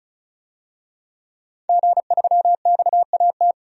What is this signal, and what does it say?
Morse code, heard as a keyed tone, about 35 wpm, 700 Hz. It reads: G3XAT
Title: DXCC-100 Call Signs - 5 Characters - 35wpm